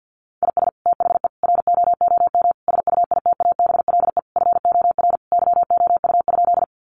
Morse code, heard as a keyed tone, about 50 wpm, 700 Hz. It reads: IS THE ROOM SUITABLE FOR YOU?
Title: Sentences from Top 1400 Words 50wpm